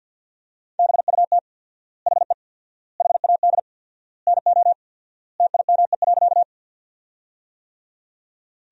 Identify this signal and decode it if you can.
Morse code, heard as a keyed tone, about 50 wpm, 700 Hz. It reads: BUT HE HAD NO TIME.
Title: 4x Word-Spacing - Sentences from Top 100 Words - 50wpm